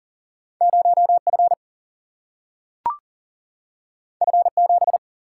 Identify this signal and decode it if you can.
Morse code, heard as a keyed tone, about 40 wpm, 700 Hz. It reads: 0 F F 7
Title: Single-Letter Single-Number 40wpm